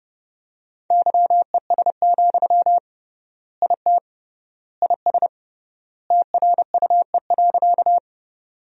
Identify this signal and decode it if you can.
Morse code, heard as a keyed tone, about 32 wpm, 700 Hz. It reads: YES, IT IS TRUE.
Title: 3x Word-Spacing - Sentences from Top 500 Words - 30wpm